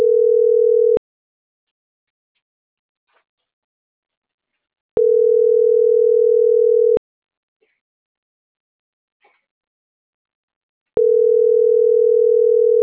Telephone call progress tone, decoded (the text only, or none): ringback tone